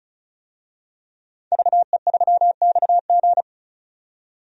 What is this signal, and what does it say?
Morse code, heard as a keyed tone, about 34 wpm, 700 Hz. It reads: VE3XG
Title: DXCC-100 Call Signs - 5 Characters - 35wpm